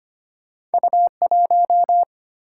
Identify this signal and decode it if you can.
Morse code, heard as a keyed tone, about 34 wpm, 700 Hz. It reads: U1